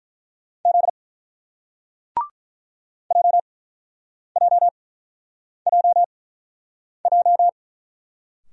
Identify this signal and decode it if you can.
Morse code, heard as a keyed tone, about 47 wpm, spaced as for 9 wpm, 700 Hz. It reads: Z JJJJ